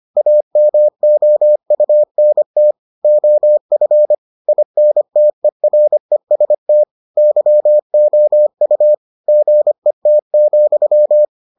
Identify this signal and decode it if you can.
Morse code, heard as a keyed tone, about 25 wpm, 600 Hz. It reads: AMOUNT OF INTEREST YOU GET,